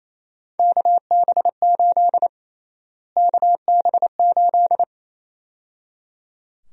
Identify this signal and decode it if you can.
Morse code, heard as a keyed tone, about 28 wpm, 700 Hz. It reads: KB8 KB8
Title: USA Call Sign Prefixes - Repeated 2x - 28wpm